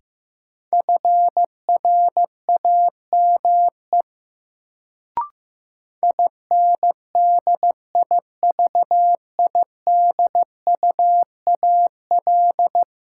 Morse code, heard as a keyed tone, about 15 wpm, 700 Hz. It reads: FRAME INDIVIDUAL
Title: Sentences from Top 1400 Words - New Word Drill Alt - 15wpm